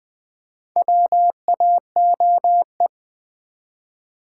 Morse code, heard as a keyed tone, about 20 wpm, 700 Hz. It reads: WAOE
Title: Four Letter-Number Combinations - 20wpm